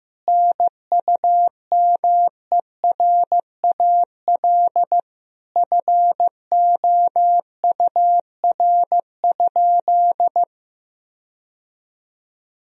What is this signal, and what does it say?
Morse code, heard as a keyed tone, about 15 wpm, 700 Hz. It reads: NUMERAL FOUR?
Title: Sentences from Top 1000 Words - Review Part 2 - 15wpm